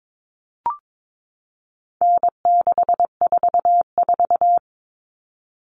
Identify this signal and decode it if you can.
Morse code, heard as a keyed tone, about 22 wpm, 700 Hz. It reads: N644
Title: Common Cut Numbers - 4-Digit 22wpm